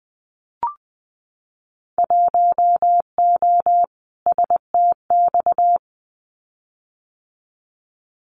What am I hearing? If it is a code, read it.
Morse code, heard as a keyed tone, about 20 wpm, 700 Hz. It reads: 1O STX
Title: Winter Field Day Exchanges 20wpm